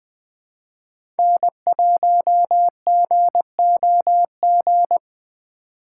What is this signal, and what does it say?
Morse code, heard as a keyed tone, about 20 wpm, 700 Hz. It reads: N1GOG
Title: USA Call Signs - 2x Character Spacing - 20wpm